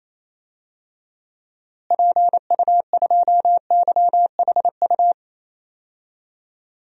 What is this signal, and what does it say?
Morse code, heard as a keyed tone, about 28 wpm, 700 Hz. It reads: PU2YHU
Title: DXCC-100 Call Signs - 6 Characters - Repeated 2x - 28wpm